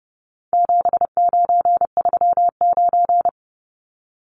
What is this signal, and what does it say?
Morse code, heard as a keyed tone, about 30 wpm, 700 Hz. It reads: 7939
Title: Numbers - 4-Digit 30wpm